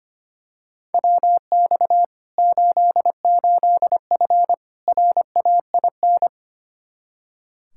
Morse code, heard as a keyed tone, about 25 wpm, 700 Hz. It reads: WX 88F RAIN
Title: Practice QSO - Exchange 2 - Info - 25wpm